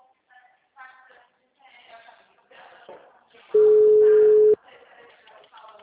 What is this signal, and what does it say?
Ringback tone (Europe, 425 Hz), with speech.